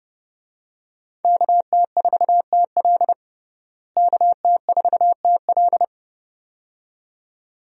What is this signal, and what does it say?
Morse code, heard as a keyed tone, about 30 wpm, 700 Hz. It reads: KT4TL KT4TL
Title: USA Call Signs - 5 Characters - Repeated 2x - 30wpm